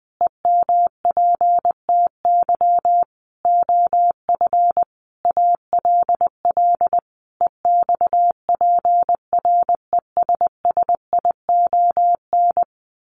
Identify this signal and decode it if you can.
Morse code, heard as a keyed tone, about 20 wpm, 700 Hz. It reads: EMPTY OF ALL EXPRESSION
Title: A Sound Of Thunder 20wpm